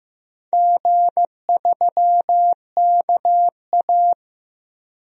Morse code, heard as a keyed tone, about 15 wpm, 700 Hz. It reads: G3KA
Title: DXCC-100 Call Signs - 4 Characters - 15wpm